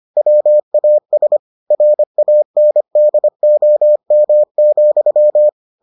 Morse code, heard as a keyed tone, about 25 wpm, 600 Hz. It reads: WAS RANDOM,